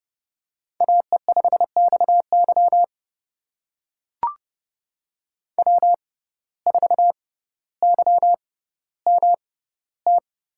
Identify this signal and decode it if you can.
Morse code, heard as a keyed tone, about 32 wpm, spaced as for 16 wpm, 700 Hz. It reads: AE5XY W4YMT
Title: USA Call Signs - 5 Characters - 6x Character Spacing - 30wpm